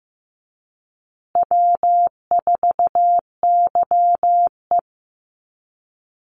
Morse code, heard as a keyed tone, about 15 wpm, 700 Hz. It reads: W4YE
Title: Call Signs Repeated 2x - 15wpm